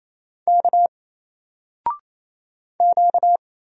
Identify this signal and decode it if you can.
Morse code, heard as a keyed tone, about 27 wpm, 700 Hz. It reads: K Q
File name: Single-Letter-Number-28wpm